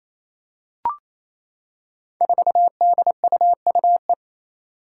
Morse code, heard as a keyed tone, about 28 wpm, 700 Hz. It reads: 4DUUE
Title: Five Challenging Letter-Number Combinations 28wpm